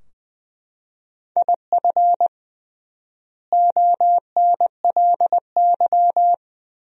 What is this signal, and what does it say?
Morse code, heard as a keyed tone, about 21 wpm, 700 Hz. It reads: IF ONLY
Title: 3x Word-Spacing - Sentences from Top 500 Words - 20wpm